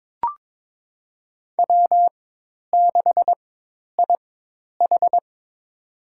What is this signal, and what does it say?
Morse code, heard as a keyed tone, about 22 wpm, spaced as for 10 wpm, 700 Hz. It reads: W6IH
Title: USA Call Signs - 4 Characters - 4x Character Spacing - 22wpm